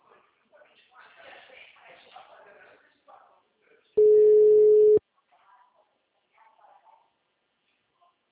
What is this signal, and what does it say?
Ringback tone (Europe, 425 Hz), with speech.
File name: out- -2002- - - .842